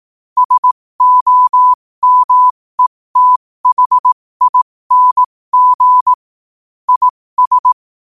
Morse code, heard as a keyed tone, about 18 wpm, spaced as for 15 wpm, 1000 Hz. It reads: SOMETHING IS
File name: qotdfiles / 15wpm